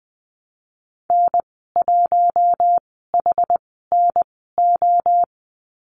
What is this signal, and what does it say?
Morse code, heard as a keyed tone, about 19 wpm, spaced as for 14 wpm, 700 Hz. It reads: N1HNO
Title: 062 Lesson 11g - Letter h - Call Signs 20wpm